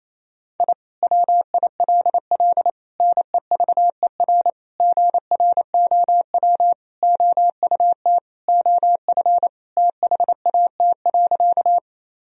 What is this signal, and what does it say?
Morse code, heard as a keyed tone, about 28 wpm, 700 Hz. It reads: I WILL NEVER GROW OUT OF THAT.
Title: Sentences from Top 1000 Words - Review Part 1 - 28wpm